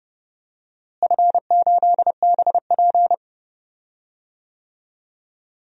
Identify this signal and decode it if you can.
Morse code, heard as a keyed tone, about 31 wpm, 700 Hz. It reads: F8BP
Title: DXCC-100 Call Signs - 4 Characters - Repeated 2x - 30wpm